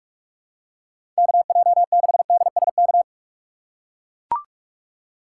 Morse code, heard as a keyed tone, about 44 wpm, 700 Hz. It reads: KJ6DSK
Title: USA Call Signs - 3x Character Spacing - 45wpm